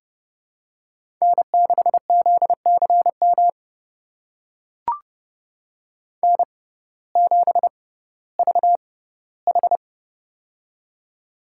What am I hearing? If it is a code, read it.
Morse code, heard as a keyed tone, about 32 wpm, spaced as for 17 wpm, 700 Hz. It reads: N6ZCM N7VH
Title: USA Call Signs - 6x Character Spacing - 30wpm